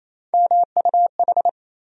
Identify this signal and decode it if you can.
Morse code, heard as a keyed tone, about 28 wpm, 700 Hz. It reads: MUH
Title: USA Call Sign Postfixes - 3 Letters 28wpm